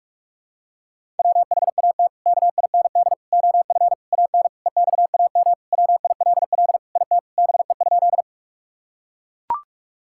Morse code, heard as a keyed tone, about 41 wpm, 700 Hz. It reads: WHAT KIND OF AN EXAM WILL IT BE?
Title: Sentences from Top 1000 Words - Review Part 5 - 45wpm